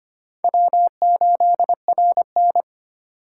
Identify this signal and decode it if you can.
Morse code, heard as a keyed tone, about 25 wpm, 700 Hz. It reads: W8RN
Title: USA Call Signs - 4 Characters - 4x Character Spacing - 25wpm